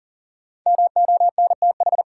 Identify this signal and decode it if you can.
Morse code, heard as a keyed tone, about 40 wpm, 700 Hz. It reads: MONTH
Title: Sentences from Top 800 Words - New Word Drill - 40wpm